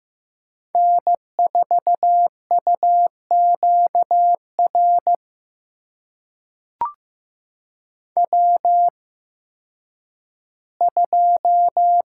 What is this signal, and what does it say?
Morse code, heard as a keyed tone, about 15 wpm, spaced as for 10 wpm, 700 Hz. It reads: N4UQR W2
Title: USA Call Signs - 5 Characters - 8x Character Spacing - 15wpm